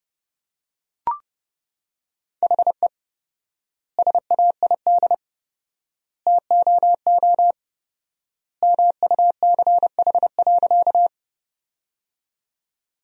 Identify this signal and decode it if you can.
Morse code, heard as a keyed tone, about 32 wpm, 700 Hz. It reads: HE SAID TOO MUCH.
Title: 4x Word-Spacing - Sentences from Top 200 Words - 30wpm